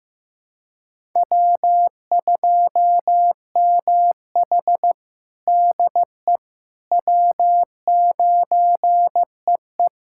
Morse code, heard as a keyed tone, about 15 wpm, 700 Hz. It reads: W2MH DE W9EE
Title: USA Callsign DE Callsign - 4 Characters - 15wpm